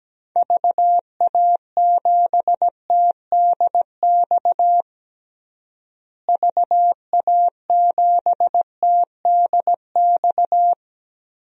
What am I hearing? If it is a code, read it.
Morse code, heard as a keyed tone, about 17 wpm, 700 Hz. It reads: VA7TDX VA7TDX
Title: DXCC-100 Call Signs - 6 Characters - Repeated 2x - 17wpm